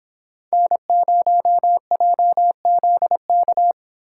Morse code, heard as a keyed tone, about 31 wpm, 700 Hz. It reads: N0JZK